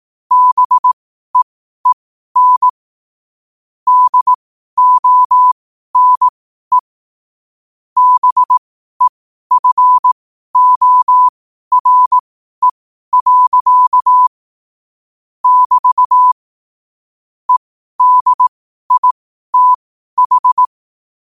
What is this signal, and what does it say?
Morse code, heard as a keyed tone, about 18 wpm, spaced as for 12 wpm, 1000 Hz. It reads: BEEN DONE BEFORE. <BT> EDITH
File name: qotdfiles / 12wpm